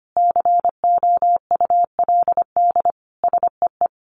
Morse code, heard as a keyed tone, about 25 wpm, 700 Hz. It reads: COULD SEE
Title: Sentences from Top 500 Words 25wpm